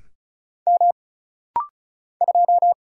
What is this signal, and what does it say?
Morse code, heard as a keyed tone, about 36 wpm, 700 Hz. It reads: M 2
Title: Single Letter-Number - Rapid-Fire 35wpm